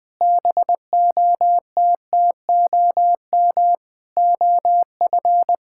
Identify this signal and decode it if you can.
Morse code, heard as a keyed tone, about 20 wpm, 700 Hz. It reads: BOTTOM OF
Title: Sentences with 7-Word Phrases - 20wpm